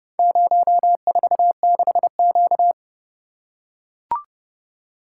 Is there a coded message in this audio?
Yes: Morse code from 0.2 to 2.7 s, heard as a keyed tone, about 29 wpm, 700 Hz. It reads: 046Q